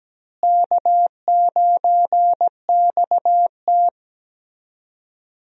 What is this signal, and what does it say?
Morse code, heard as a keyed tone, about 17 wpm, 700 Hz. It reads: K9XT
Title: USA Call Signs - 4 Characters - Repeated 2x - 17wpm